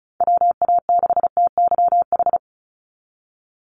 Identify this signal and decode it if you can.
Morse code, heard as a keyed tone, about 36 wpm, 700 Hz. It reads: WA6TYH
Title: USA Call Signs 35wpm